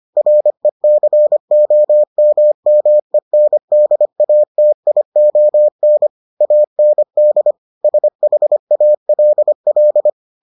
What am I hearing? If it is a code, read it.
Morse code, heard as a keyed tone, about 25 wpm, 600 Hz. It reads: RECOMMENDATION AND SHALL